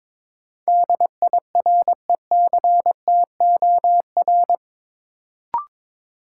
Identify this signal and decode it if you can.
Morse code, heard as a keyed tone, about 22 wpm, 700 Hz. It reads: DIRECTOR
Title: Sentences from Top 1300 Words - New Word Drill Alt - 22wpm